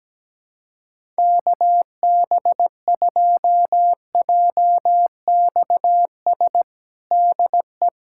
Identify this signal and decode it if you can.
Morse code, heard as a keyed tone, about 17 wpm, 700 Hz. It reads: KB2JXS DE
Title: USA Callsign DE Callsign 17wpm